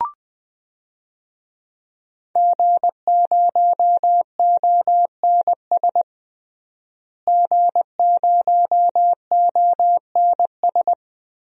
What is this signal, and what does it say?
Morse code, heard as a keyed tone, about 20 wpm, 700 Hz. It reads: G0ONS G0ONS
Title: DXCC-100 Call Signs - 5 Characters - Repeated 3x - 20wpm